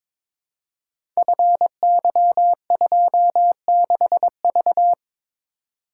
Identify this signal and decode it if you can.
Morse code, heard as a keyed tone, about 22 wpm, 700 Hz. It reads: FY26V
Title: Code Groups - 5 Letter:Numbers - 22wpm